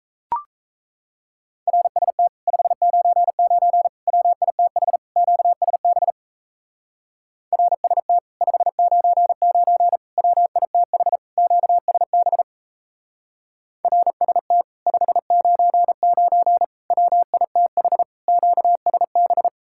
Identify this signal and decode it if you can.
Morse code, heard as a keyed tone, about 37 wpm, 700 Hz. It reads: RST 599 WITH QSB RST 599 WITH QSB RST 599 WITH QSB